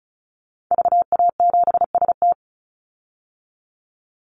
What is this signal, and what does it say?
Morse code, heard as a keyed tone, about 34 wpm, 700 Hz. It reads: VA7ST
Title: Call Signs Repeated 3x - 35wpm